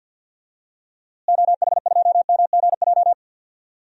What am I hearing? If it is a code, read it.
Morse code, heard as a keyed tone, about 51 wpm, 700 Hz. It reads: OH2MGJ
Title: DXCC-100 Call Signs - 6 Characters - Repeated 3x - 50wpm